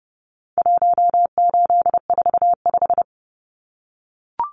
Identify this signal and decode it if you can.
Morse code, heard as a keyed tone, about 30 wpm, 700 Hz. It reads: 1845
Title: Numbers - 4-Digit 30wpm